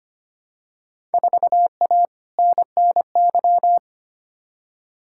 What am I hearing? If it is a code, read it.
Morse code, heard as a keyed tone, about 25 wpm, 700 Hz. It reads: 4A NNY
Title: Field Day Exchanges Encore - 25wpm